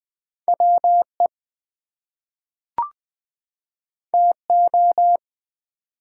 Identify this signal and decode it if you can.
Morse code, heard as a keyed tone, about 20 wpm, 700 Hz. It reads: WE TO
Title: Two Letter Words - 20wpm